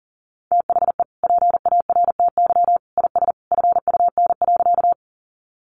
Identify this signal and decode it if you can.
Morse code, heard as a keyed tone, about 40 wpm, 700 Hz. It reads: THE PARTY IS FUN.
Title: Easy Phrases 40wpm